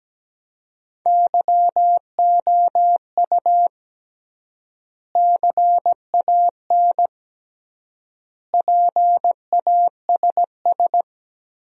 Morse code, heard as a keyed tone, about 17 wpm, 700 Hz. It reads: YOU CAN PASS
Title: 3x Word-Spacing - Sentences from Top 400 Words - 17wpm